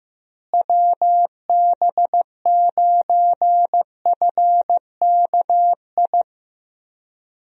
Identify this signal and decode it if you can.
Morse code, heard as a keyed tone, about 15 wpm, 700 Hz. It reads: WB9FKI
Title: USA Call Signs - 6 Characters - Repeated 3x - 15wpm